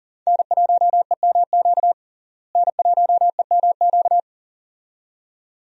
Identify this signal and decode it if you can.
Morse code, heard as a keyed tone, about 38 wpm, 700 Hz. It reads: N1EMQ N1EMQ
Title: USA Call Signs - 5 Characters - Repeated 2x - 40wpm